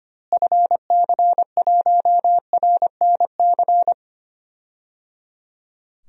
Morse code, heard as a keyed tone, about 25 wpm, 700 Hz. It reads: FC1RNC
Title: DXCC-100 Call Signs - 6 Characters - Repeated 3x - 25wpm